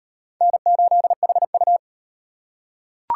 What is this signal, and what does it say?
Morse code, heard as a keyed tone, about 38 wpm, 700 Hz. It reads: N8HU